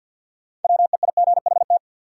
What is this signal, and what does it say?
Morse code, heard as a keyed tone, about 53 wpm, 700 Hz. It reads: WEIGHT
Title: Sentences from Top 800 Words - New Word Drill - 50wpm